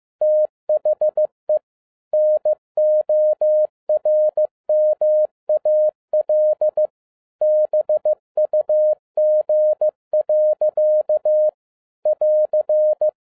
Morse code, heard as a keyed tone, about 15 wpm, 610 Hz. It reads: THE NORMAL BUG. <AR>